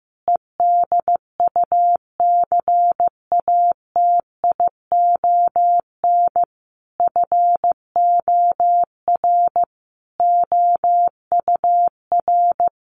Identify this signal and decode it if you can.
Morse code, heard as a keyed tone, about 15 wpm, 700 Hz. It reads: EDUCATION FOR OUR
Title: Sentences from Top 1400 Words 15wpm